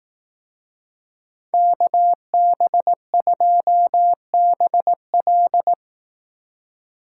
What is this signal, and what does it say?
Morse code, heard as a keyed tone, about 18 wpm, 700 Hz. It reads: KB2BL
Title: Speed Racing - USA Call Signs 15wpm